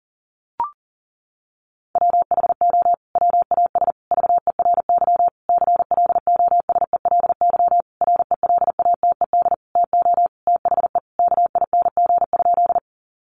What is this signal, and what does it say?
Morse code, heard as a keyed tone, about 37 wpm, 700 Hz. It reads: WHO WAS VERY CLOSELY RELATED TO THE KING?
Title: Sentences from Top 1400 Words 40wpm